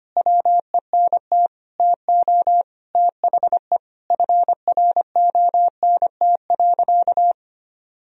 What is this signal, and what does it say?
Morse code, heard as a keyed tone, about 25 wpm, 700 Hz. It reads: WENT TO THE FRONT.
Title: Sentences from Top 1000 Words - Review Part 2 - 25wpm